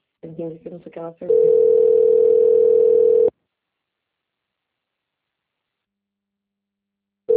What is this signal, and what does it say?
Ringback tone (North America), 440 + 480 Hz, with speech.
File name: out- - - - - .13337